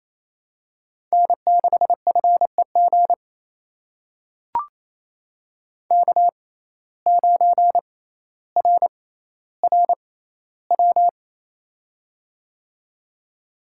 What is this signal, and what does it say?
Morse code, heard as a keyed tone, about 28 wpm, spaced as for 14 wpm, 700 Hz. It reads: N6FEG K9RRW
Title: USA Call Signs - 5 Characters - 6x Character Spacing - 28wpm